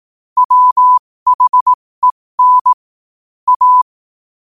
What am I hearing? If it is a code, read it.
Morse code, heard as a keyed tone, about 18 wpm, spaced as for 15 wpm, 1000 Hz. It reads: WHEN A
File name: qotdfiles / 15wpm